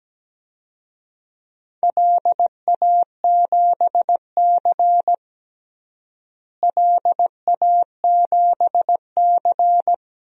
Morse code, heard as a keyed tone, about 17 wpm, 700 Hz. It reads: LA7C LA7C
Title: DXCC-100 Call Signs - 4 Characters - Repeated 2x - 17wpm